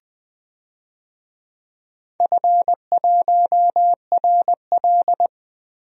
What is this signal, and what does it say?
Morse code, heard as a keyed tone, about 20 wpm, 700 Hz. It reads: F1RL
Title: DXCC-100 Call Signs - 4 Characters - Repeated 2x - 20wpm